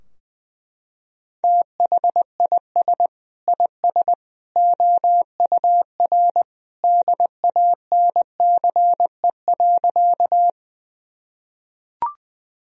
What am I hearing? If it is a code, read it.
Morse code, heard as a keyed tone, about 20 wpm, 700 Hz. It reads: THIS IS OUR DANCE.
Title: Sentences from Top 1000 Words - Review Part 4 - 20wpm